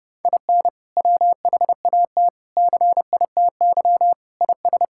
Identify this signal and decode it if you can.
Morse code, heard as a keyed tone, about 30 wpm, 700 Hz. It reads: IN WHAT CITY IS
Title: Sentences from Top 1000 Words - Review Part 6 - 30wpm